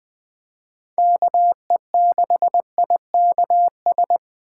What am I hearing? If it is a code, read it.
Morse code, heard as a keyed tone, about 20 wpm, 700 Hz. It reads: KE6IKS
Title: Speed Racing - USA 2x3 Call Signs 17wpm